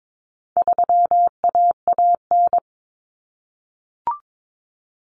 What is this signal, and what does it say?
Morse code, heard as a keyed tone, about 22 wpm, 700 Hz. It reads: 3AAN
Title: Common Cut Numbers - 4-Digit 22wpm